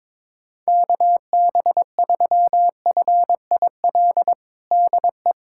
Morse code, heard as a keyed tone, about 22 wpm, 700 Hz. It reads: KB3FIL DE
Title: USA Callsign DE Callsign - 6 Characters - 22wpm